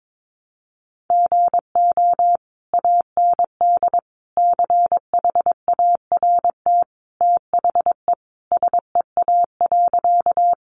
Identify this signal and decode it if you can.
Morse code, heard as a keyed tone, about 22 wpm, 700 Hz. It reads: GO AND CHART THE SEA.
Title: Sentences from Top 1300 Words 22wpm